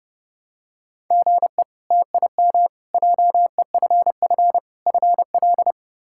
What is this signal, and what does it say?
Morse code, heard as a keyed tone, about 31 wpm, 700 Hz. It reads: GE TIM JEFF FL
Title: K1USN SST QSO Practice - 30wpm